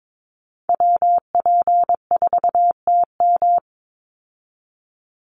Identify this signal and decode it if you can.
Morse code, heard as a keyed tone, about 22 wpm, 700 Hz. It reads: WP4TM
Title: USA Call Signs 22wpm